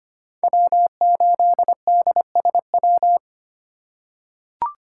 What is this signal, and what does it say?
Morse code, heard as a keyed tone, about 25 wpm, 700 Hz. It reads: W8DSW
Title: USA Call Signs - 5 Characters - 3x Character Spacing - 25wpm